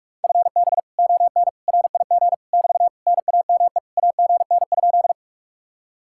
Morse code, heard as a keyed tone, about 50 wpm, 700 Hz. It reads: FB ON RIG <BT> NAME AGN?